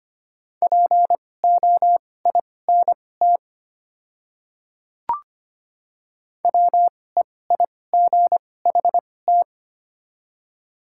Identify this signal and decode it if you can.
Morse code, heard as a keyed tone, about 25 wpm, spaced as for 18 wpm, 700 Hz. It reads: POINT WEIGHT
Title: 2x Character Spacing - Top 500 Words - 25wpm